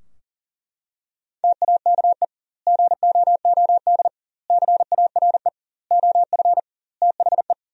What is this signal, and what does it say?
Morse code, heard as a keyed tone, about 40 wpm, 700 Hz. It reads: TAKE GOOD CARE OF THE
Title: 2x Word-Spacing - Sentences from Top 400 Words - 40wpm